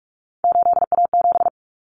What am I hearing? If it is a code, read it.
Morse code, heard as a keyed tone, about 44 wpm, 700 Hz. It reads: 8A7
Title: Common Cut Numbers - 3-Digit 45wpm